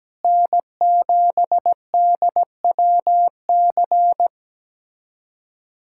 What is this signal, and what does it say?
Morse code, heard as a keyed tone, about 17 wpm, 700 Hz. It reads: N7DWC
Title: USA Callsign DE Callsign 17wpm